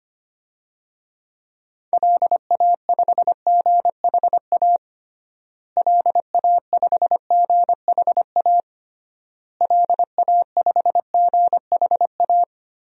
Morse code, heard as a keyed tone, about 25 wpm, 700 Hz. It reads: LA5GHA LA5GHA LA5GHA
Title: DXCC-100 Call Signs - 6 Characters - Repeated 3x - 25wpm